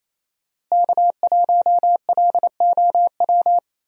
Morse code, heard as a keyed tone, about 28 wpm, 700 Hz. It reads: K1LOW
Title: USA Call Signs - 5 Characters - Repeated 3x - 28wpm